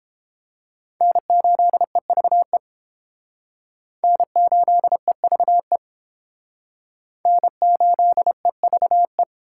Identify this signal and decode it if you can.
Morse code, heard as a keyed tone, about 29 wpm, 700 Hz. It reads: N8EVE N8EVE N8EVE